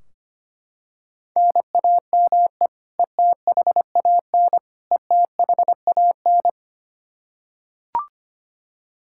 Morse code, heard as a keyed tone, about 25 wpm, 700 Hz. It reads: NAME ETHAN ETHAN
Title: Practice QSO - Exchange 1 - Beginning - 25wpm